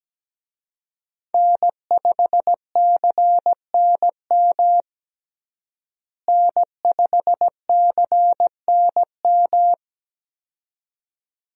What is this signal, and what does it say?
Morse code, heard as a keyed tone, about 17 wpm, 700 Hz. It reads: N5CNM N5CNM
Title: USA Call Signs - 5 Characters - Repeated 2x - 17wpm